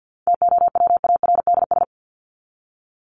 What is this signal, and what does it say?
Morse code, heard as a keyed tone, about 48 wpm, 700 Hz. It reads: TOWARDS
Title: ES:S Suffix - Top 100 Words with Base Words 50wpm